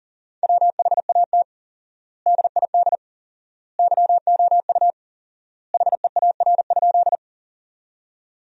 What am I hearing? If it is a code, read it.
Morse code, heard as a keyed tone, about 37 wpm, 700 Hz. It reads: WHAT DID YOU HEAR?
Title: 4x Word-Spacing - Sentences from Top 300 Words - 40wpm